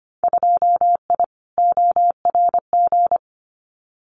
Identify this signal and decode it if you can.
Morse code, heard as a keyed tone, about 25 wpm, 700 Hz. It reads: 2I ORG
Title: Winter Field Day Exchanges Encore 25wpm